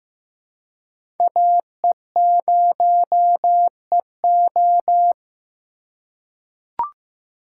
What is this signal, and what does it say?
Morse code, heard as a keyed tone, about 15 wpm, 700 Hz. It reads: AE0EO
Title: USA Call Signs - 5 Characters - 3x Character Spacing - 15wpm